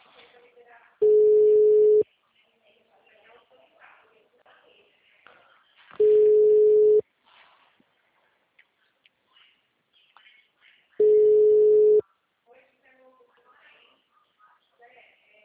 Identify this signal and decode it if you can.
Ringback tone (Europe, 425 Hz), with speech.